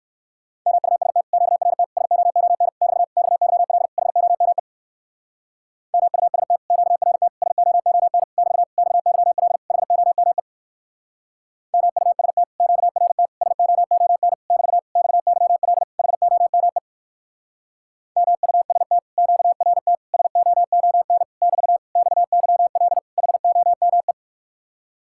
Morse code, heard as a keyed tone, about 56 wpm, 700 Hz. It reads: MUST QRT SOON <BT> XYL HOME MUST QRT SOON <BT> XYL HOME MUST QRT SOON <BT> XYL HOME MUST QRT SOON <BT> XYL HOME
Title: Speed-Racing Common QSO Phrases Encore - 45wpm